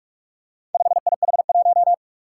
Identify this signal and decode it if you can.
Morse code, heard as a keyed tone, about 44 wpm, 700 Hz. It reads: 5IH1
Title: Four Challenging Letter-Number Combinations 45wpm